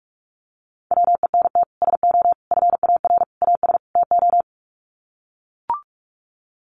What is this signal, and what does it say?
Morse code, heard as a keyed tone, about 44 wpm, 700 Hz. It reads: WENT SO FAR AS TO
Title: Sets of 5 Words Encore 45wpm